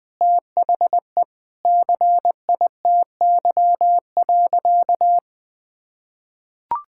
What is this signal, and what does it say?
Morse code, heard as a keyed tone, about 20 wpm, 700 Hz. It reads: THE CITY.
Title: Sentences with 6-Word Phrases - 20wpm